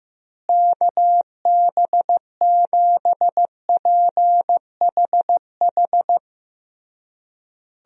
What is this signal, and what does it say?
Morse code, heard as a keyed tone, about 16 wpm, 700 Hz. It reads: KB7PHH
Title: USA Call Signs - 6 Characters - Repeated 3x - 15wpm